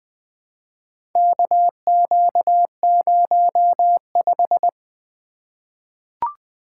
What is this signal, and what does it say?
Morse code, heard as a keyed tone, about 20 wpm, 700 Hz. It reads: KQ05
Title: Four Letter-Number Combinations - 20wpm